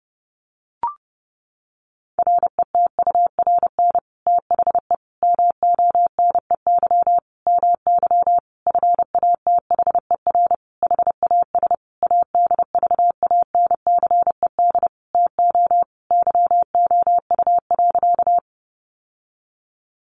Morse code, heard as a keyed tone, about 30 wpm, 700 Hz. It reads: RETURN THE MONEY MY FATHER HAS ADVANCED TO YOU.
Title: Sentences from Top 1200 Words 30wpm